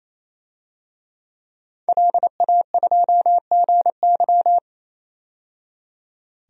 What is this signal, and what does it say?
Morse code, heard as a keyed tone, about 28 wpm, 700 Hz. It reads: LA2GY
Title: DXCC-100 Call Signs - 5 Characters - 28wpm